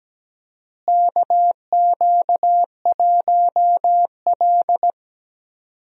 Morse code, heard as a keyed tone, about 17 wpm, 700 Hz. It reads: KQ1L
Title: USA Call Signs - 4 Characters - Repeated 3x - 17wpm